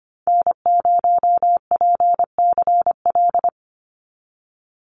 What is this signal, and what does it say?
Morse code, heard as a keyed tone, about 25 wpm, 690 Hz. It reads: N0PCL
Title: NA SOTA Activator Callsigns 25wpm